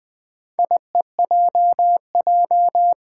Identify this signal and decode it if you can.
Morse code, heard as a keyed tone, about 20 wpm, 700 Hz. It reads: IEJJ
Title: Four Challenging Letter-Number Combinations 20wpm